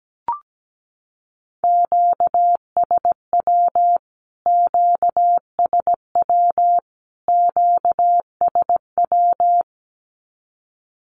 Morse code, heard as a keyed tone, about 17 wpm, 700 Hz. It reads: QSW QSW QSW
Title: Q-Codes Repeated 3x 17wpm